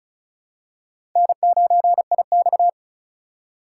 Morse code, heard as a keyed tone, about 36 wpm, 700 Hz. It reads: N9IX
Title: USA Call Signs - 4 Characters - 4x Character Spacing - 35wpm